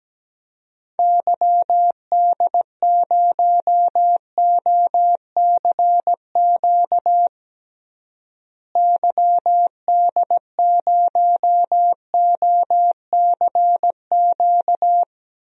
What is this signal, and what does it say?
Morse code, heard as a keyed tone, about 17 wpm, 700 Hz. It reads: YD0OCQ YD0OCQ
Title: DXCC-100 Call Signs - 6 Characters - Repeated 3x - 17wpm